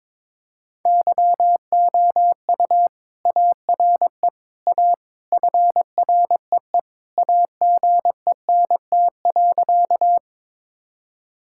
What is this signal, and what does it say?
Morse code, heard as a keyed tone, about 21 wpm, 700 Hz. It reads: YOU ARE A FREE AGENT.
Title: Sentences from Top 1600 Words 22wpm